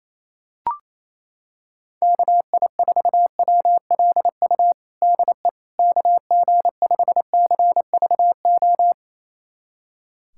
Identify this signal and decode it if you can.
Morse code, heard as a keyed tone, about 27 wpm, 700 Hz. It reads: KI4WLU DE KG5CVO
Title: USA Callsign DE Callsign - 6 Characters - 28wpm